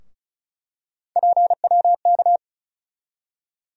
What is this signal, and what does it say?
Morse code, heard as a keyed tone, about 36 wpm, 700 Hz. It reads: PWK
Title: USA Call Sign Postfixes - 3 Letters 35wpm